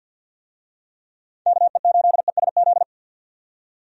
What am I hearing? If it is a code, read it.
Morse code, heard as a keyed tone, about 51 wpm, 700 Hz. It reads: KE8ESZ
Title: USA Call Signs - 6 Characters - 50wpm